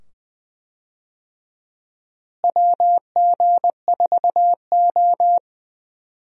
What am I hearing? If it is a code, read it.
Morse code, heard as a keyed tone, about 21 wpm, 700 Hz. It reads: WG4O
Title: USA Call Signs - 4 Characters - Repeated 2x - 20wpm